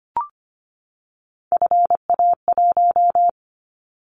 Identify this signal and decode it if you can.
Morse code, heard as a keyed tone, about 25 wpm, 700 Hz. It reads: FA1
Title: Callsign Prefixes for Top 50 DXCC - 25wpm